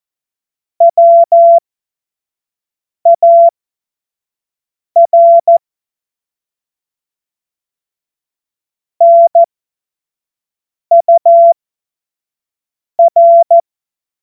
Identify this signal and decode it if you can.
Morse code, heard as a keyed tone, about 14 wpm, spaced as for 5 wpm, 680 Hz. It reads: WAR NUR
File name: Morsekurs / Witze / Witz 13. April 2021, 25 F70